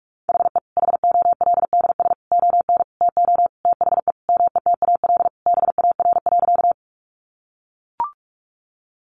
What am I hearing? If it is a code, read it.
Morse code, heard as a keyed tone, about 44 wpm, 700 Hz. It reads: HE HOLDS ON TO THE METAL BAR.
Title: Additional Easy Phrases 45wpm